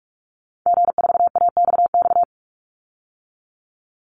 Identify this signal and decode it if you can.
Morse code, heard as a keyed tone, about 44 wpm, 700 Hz. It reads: G4AXX
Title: Call Signs - 45wpm